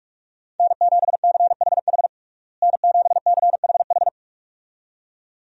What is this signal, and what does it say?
Morse code, heard as a keyed tone, about 42 wpm, 700 Hz. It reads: N7CHH N7CHH
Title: USA Call Signs - 5 Characters - Repeated 2x - 45wpm